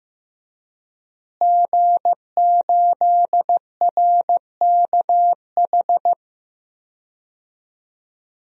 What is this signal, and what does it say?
Morse code, heard as a keyed tone, about 15 wpm, 700 Hz. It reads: G8RKH
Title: DXCC-100 Call Signs - 5 Characters - Repeated 2x - 15wpm